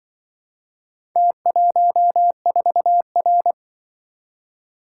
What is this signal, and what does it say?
Morse code, heard as a keyed tone, about 25 wpm, 700 Hz. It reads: T14R